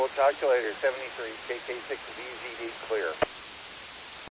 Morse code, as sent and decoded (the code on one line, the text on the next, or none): none